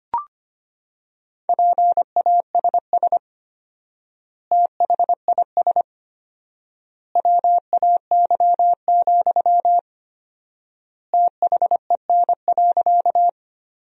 Morse code, heard as a keyed tone, about 25 wpm, 700 Hz. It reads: PASS THIS WAY, THEN.
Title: 4x Word-Spacing - Sentences from Top 400 Words - 25wpm